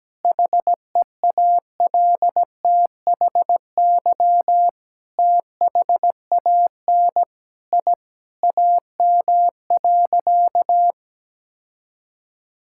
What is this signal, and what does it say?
Morse code, heard as a keyed tone, about 17 wpm, 700 Hz. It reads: HEALTHY THAN I AM.
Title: Sentences from Top 1000 Words - Review Part 5 - 17wpm